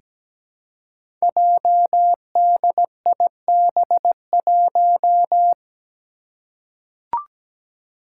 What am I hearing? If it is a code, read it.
Morse code, heard as a keyed tone, about 17 wpm, 700 Hz. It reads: JDIB1
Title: Five Challenging Letter-Number Combinations 17wpm